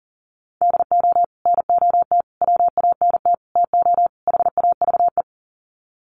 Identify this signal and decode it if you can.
Morse code, heard as a keyed tone, about 40 wpm, 700 Hz. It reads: DO NOT WANT TO HAVE
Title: Sets of 5 Words 40wpm